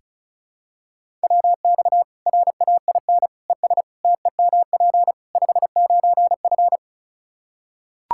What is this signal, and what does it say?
Morse code, heard as a keyed tone, about 34 wpm, 700 Hz. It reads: WX RAIN ES TEMP 59F
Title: Practice QSO - Exchange 2 - Info - 35wpm